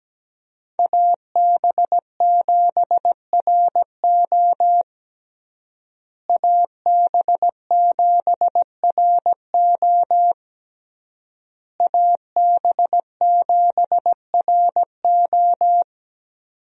Morse code, heard as a keyed tone, about 17 wpm, 700 Hz. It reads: AB7RO AB7RO AB7RO
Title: USA Call Signs - 5 Characters - Repeated 3x - 17wpm